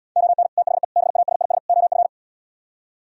Morse code, heard as a keyed tone, about 77 wpm, 700 Hz. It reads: ON THE BASIS OF